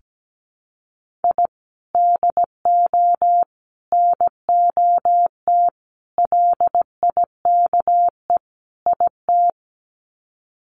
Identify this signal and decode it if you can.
Morse code, heard as a keyed tone, about 17 wpm, 700 Hz. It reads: I DO NOT LIKE IT
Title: Sets of 5 Words Encore 17wpm